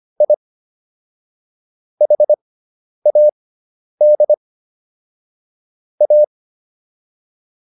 Morse code, heard as a keyed tone, about 25 wpm, spaced as for 10 wpm, 600 Hz. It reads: I HAD A